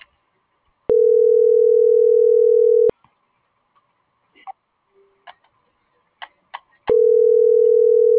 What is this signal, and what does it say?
Ringback tone (North America), 440 + 480 Hz.
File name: CENTRAL GENERAL / out karla.sky - .68530